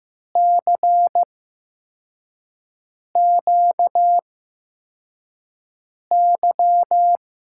Morse code, heard as a keyed tone, about 15 wpm, spaced as for 4 wpm, 700 Hz. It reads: CQY
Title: USA Call Signs - 6 Characters - 8x Character Spacing - 15wpm